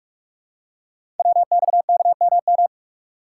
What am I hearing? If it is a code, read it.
Morse code, heard as a keyed tone, about 44 wpm, 700 Hz. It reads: WXKMM
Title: Code Groups - 5 Letters - 45wpm